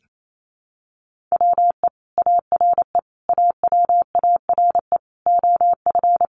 Morse code, heard as a keyed tone, about 28 wpm, 700 Hz. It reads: WE ARE AWARE OF
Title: Sentences from Top 1200 Words 28wpm